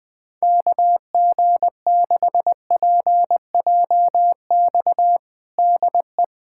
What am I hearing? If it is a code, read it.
Morse code, heard as a keyed tone, about 20 wpm, 700 Hz. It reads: KG6PJX DE
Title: USA Callsign DE Callsign - 6 Characters - 20wpm